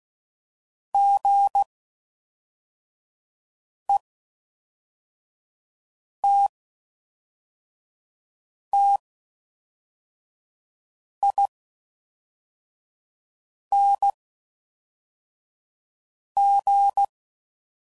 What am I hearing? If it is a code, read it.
Morse code, heard as a keyed tone, about 16 wpm, spaced as for 4 wpm, 800 Hz. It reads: GETTING
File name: Mixed / Short Runs / Gotten